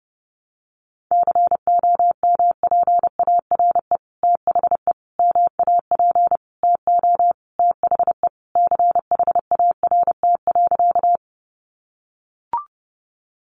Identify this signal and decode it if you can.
Morse code, heard as a keyed tone, about 30 wpm, 700 Hz. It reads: COMPARE THE MAP TO THE CHART.
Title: Sentences from Top 1300 Words 30wpm